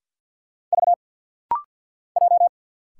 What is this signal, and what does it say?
Morse code, heard as a keyed tone, about 50 wpm, 700 Hz. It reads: V J
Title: Single Letter-Number Alt - Rapid-Fire 50wpm